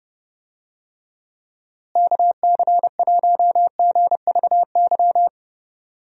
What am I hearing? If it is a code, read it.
Morse code, heard as a keyed tone, about 30 wpm, 700 Hz. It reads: KC1GVY
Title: USA Call Signs - 6 Characters - Repeated 2x - 30wpm